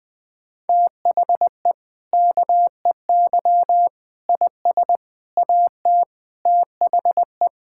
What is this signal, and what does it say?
Morse code, heard as a keyed tone, about 20 wpm, 700 Hz. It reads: THE KEY IS AT THE
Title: Sentences with 5-Word Phrases - 20wpm